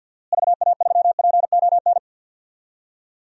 Morse code, heard as a keyed tone, about 49 wpm, 700 Hz. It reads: VA3PON
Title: DXCC-100 Call Signs - 6 Characters - Repeated 3x - 50wpm